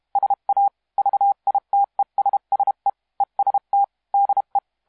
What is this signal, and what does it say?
Morse code, heard as a keyed tone, about 31 wpm, 800 Hz. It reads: SA VITESSE EST DE
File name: musiqueradio / cw hsc demo f6ddr